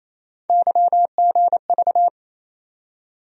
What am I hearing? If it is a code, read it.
Morse code, heard as a keyed tone, about 28 wpm, 700 Hz. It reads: YGV
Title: USA Call Sign Postfixes - 3 Letters 28wpm